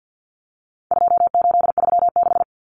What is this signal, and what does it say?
Morse code, heard as a keyed tone, about 50 wpm, 700 Hz. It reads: 2836
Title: Common Cut Numbers - 4-Digit 50wpm